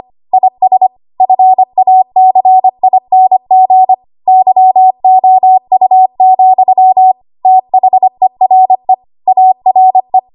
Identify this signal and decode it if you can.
Morse code, heard as a keyed tone, about 25 wpm, 750 Hz. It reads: IS FACING YOU, THERE ARE
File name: Archive / 25 WPM / 25WPM